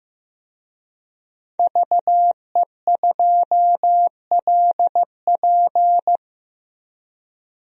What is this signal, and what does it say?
Morse code, heard as a keyed tone, about 15 wpm, 700 Hz. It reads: VE2LP
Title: DXCC-100 Call Signs - 5 Characters - 15wpm